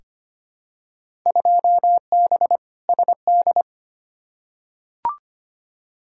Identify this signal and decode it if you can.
Morse code, heard as a keyed tone, about 25 wpm, 700 Hz. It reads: 2B SD
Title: Field Day Exchanges Encore - 25wpm